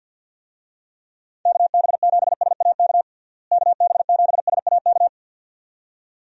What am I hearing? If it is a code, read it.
Morse code, heard as a keyed tone, about 50 wpm, 700 Hz. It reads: KB7SAK KB7SAK
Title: USA Call Signs - 6 Characters - Repeated 2x - 50wpm